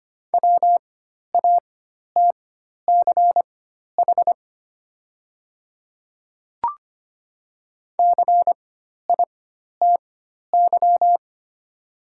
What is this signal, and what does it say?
Morse code, heard as a keyed tone, about 25 wpm, spaced as for 12 wpm, 700 Hz. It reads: WATCH CITY
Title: 4x Character Spacing - Top 500 Words - 25wpm